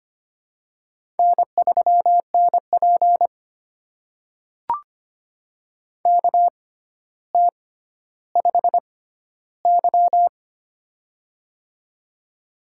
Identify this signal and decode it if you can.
Morse code, heard as a keyed tone, about 25 wpm, spaced as for 13 wpm, 700 Hz. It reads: N3NP KT5Y
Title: USA Call Signs - 4 Characters - 6x Character Spacing - 25wpm